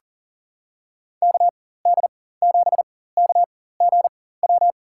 Morse code, heard as a keyed tone, about 40 wpm, spaced as for 19 wpm, 700 Hz. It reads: KD7KGW
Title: USA Call Signs - 4x Character Spacing - 40wpm